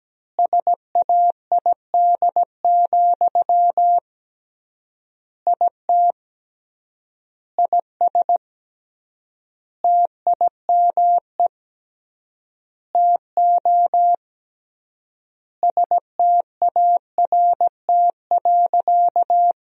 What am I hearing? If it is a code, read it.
Morse code, heard as a keyed tone, about 17 wpm, 700 Hz. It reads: SAID, IT IS TIME TO START.
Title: 3x Word-Spacing - Sentences from Top 300 Words - 17wpm